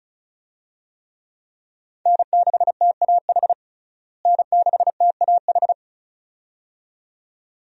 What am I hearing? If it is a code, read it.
Morse code, heard as a keyed tone, about 34 wpm, 700 Hz. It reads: N6TAH N6TAH
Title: USA Call Signs - 5 Characters - Repeated 2x - 35wpm